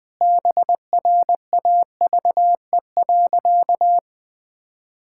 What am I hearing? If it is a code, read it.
Morse code, heard as a keyed tone, about 20 wpm, 700 Hz. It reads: BRAVE.
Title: Sentences with 5-Word Phrases - 20wpm